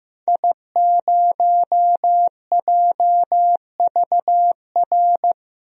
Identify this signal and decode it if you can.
Morse code, heard as a keyed tone, about 15 wpm, 700 Hz. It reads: I0JVR
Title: DXCC-100 Call Signs - 5 Characters - Repeated 3x - 15wpm